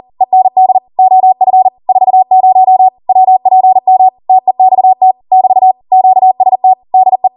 Morse code, heard as a keyed tone, about 40 wpm, 750 Hz. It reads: END OF 40 WPM TEXT <BT> QST DE